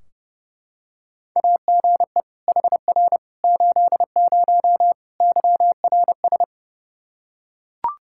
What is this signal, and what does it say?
Morse code, heard as a keyed tone, about 30 wpm, 700 Hz. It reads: AGE HR 80 YRS
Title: K1USN SST QSO Practice 30wpm